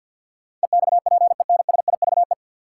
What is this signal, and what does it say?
Morse code, heard as a keyed tone, about 50 wpm, 700 Hz. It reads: EXPENSIVE
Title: Sentences from Top 1500 Words - New Word Drill Alt - 50wpm